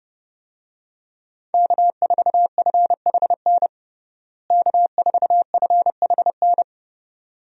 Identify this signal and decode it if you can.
Morse code, heard as a keyed tone, about 30 wpm, 700 Hz. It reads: K4FHN K4FHN
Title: USA Call Signs - 5 Characters - Repeated 2x - 30wpm